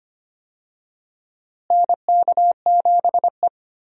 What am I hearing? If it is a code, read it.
Morse code, heard as a keyed tone, about 25 wpm, 700 Hz. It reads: NK7E
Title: USA Call Signs - 4 Characters - Repeated 2x - 25wpm